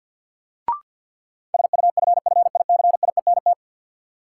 Morse code, heard as a keyed tone, about 45 wpm, 700 Hz. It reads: SUFFICIENT
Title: Sentences from Top 1400 Words - New Word Drill Alt - 50wpm